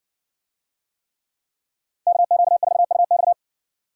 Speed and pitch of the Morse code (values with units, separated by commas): 60 wpm, 700 Hz